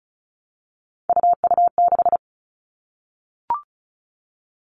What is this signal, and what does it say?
Morse code, heard as a keyed tone, about 36 wpm, 700 Hz. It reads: UU6
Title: Three Challenging Letter-Number Combinations 35wpm